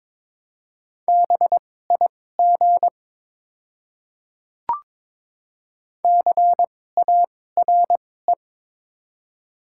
Morse code, heard as a keyed tone, about 22 wpm, spaced as for 16 wpm, 690 Hz. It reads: BIG CARE
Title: 2x Character Spacing - Top 500 Words - 22wpm